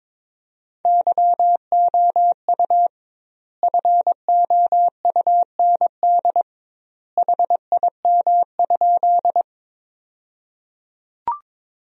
Morse code, heard as a keyed tone, about 21 wpm, 700 Hz. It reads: YOU FOUND HIM?
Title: 2x Word-Spacing - Sentences from Top 300 Words - 22wpm